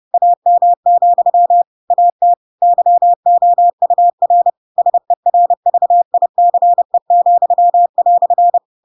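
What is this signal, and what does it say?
Morse code, heard as a keyed tone, about 30 wpm, 700 Hz. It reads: AM, AT YOUR SERVICE,"
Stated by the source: Chapter 2 of The Adventures of Pinocchio read as a book in Morse code (CW) at 30 wpm, by ik1hge.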